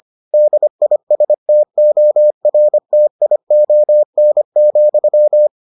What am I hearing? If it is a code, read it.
Morse code, heard as a keyed tone, about 25 wpm, 600 Hz. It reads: DISTORTION,